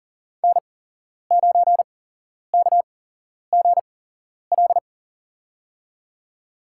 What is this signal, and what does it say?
Morse code, heard as a keyed tone, about 40 wpm, spaced as for 11 wpm, 700 Hz. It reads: N9KGL
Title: USA Call Signs - 8x Character Spacing - 40wpm